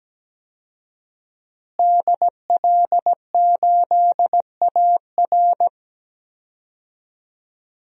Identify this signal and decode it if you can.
Morse code, heard as a keyed tone, about 17 wpm, 700 Hz. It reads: DL8AR
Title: DXCC-100 Call Signs - 5 Characters - Repeated 3x - 17wpm